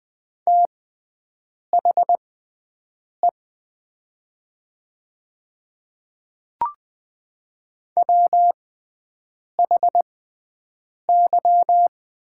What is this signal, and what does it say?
Morse code, heard as a keyed tone, about 20 wpm, spaced as for 7 wpm, 700 Hz. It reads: THE WHY
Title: 6x Character Spacing - Top 500 Words - 20wpm